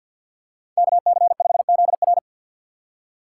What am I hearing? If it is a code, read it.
Morse code, heard as a keyed tone, about 48 wpm, 700 Hz. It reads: KC5ZR
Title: USA Call Signs - 5 Characters - 6x Character Spacing - 50wpm